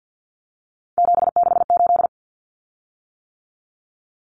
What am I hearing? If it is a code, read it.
Morse code, heard as a keyed tone, about 48 wpm, 700 Hz. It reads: 768
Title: Common Cut Numbers - 3-Digit 50wpm